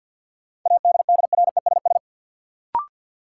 Morse code, heard as a keyed tone, about 53 wpm, 700 Hz. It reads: ADDRESS
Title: Sentences from Top 1000 Words - New Word Drill - 50wpm